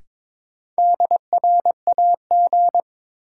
Morse code, heard as a keyed tone, about 21 wpm, 700 Hz. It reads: DRAG
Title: Sentences from Top 1500 Words - New Word Drill - 22wpm